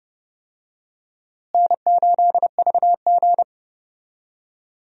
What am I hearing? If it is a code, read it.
Morse code, heard as a keyed tone, about 30 wpm, 700 Hz. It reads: N8VG
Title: USA Call Signs - 4 Characters - Repeated 2x - 30wpm